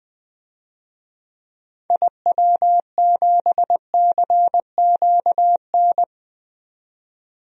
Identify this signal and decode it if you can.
Morse code, heard as a keyed tone, about 21 wpm, 700 Hz. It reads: IW7CQN
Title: DXCC-100 Call Signs - 6 Characters - 20wpm